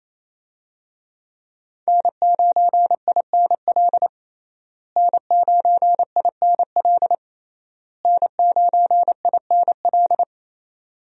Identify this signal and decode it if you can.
Morse code, heard as a keyed tone, about 28 wpm, 700 Hz. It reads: N9INL N9INL N9INL
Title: USA Call Signs - 5 Characters - Repeated 3x - 28wpm